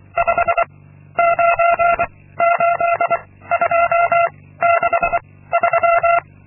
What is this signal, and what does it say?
Morse code, heard as a keyed tone, about 25 wpm, 680 Hz. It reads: 5 9 8 2 6 3
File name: Linksmakalnis-2